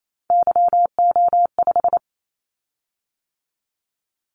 Morse code, heard as a keyed tone, about 28 wpm, 700 Hz. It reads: YO5
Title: Callsign Prefixes for Top 25 DXCC - 28wpm